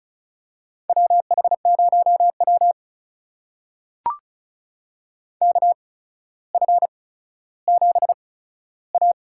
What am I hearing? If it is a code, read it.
Morse code, heard as a keyed tone, about 34 wpm, spaced as for 15 wpm, 700 Hz. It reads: WH0W KF7A
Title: USA Call Signs - 8x Character Spacing - 35wpm